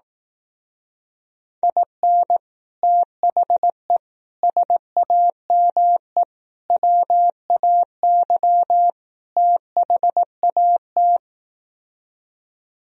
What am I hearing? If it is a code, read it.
Morse code, heard as a keyed tone, about 18 wpm, 700 Hz. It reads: IN THE SAME WAY THAT
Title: Speed-Racing - Sets of 5 Words 15wpm